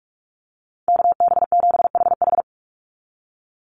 Morse code, heard as a keyed tone, about 44 wpm, 700 Hz. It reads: KB7HH
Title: NA SOTA Chaser Callsigns 45wpm